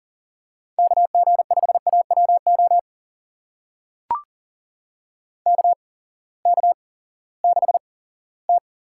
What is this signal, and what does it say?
Morse code, heard as a keyed tone, about 39 wpm, spaced as for 20 wpm, 700 Hz. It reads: KG5AWO KK6T
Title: USA Call Signs - 6 Characters - 8x Character Spacing - 40wpm